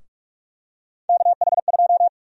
Morse code, heard as a keyed tone, about 46 wpm, 700 Hz. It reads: KH2
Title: USA Call Sign Prefixes - Repeated 2x - 45wpm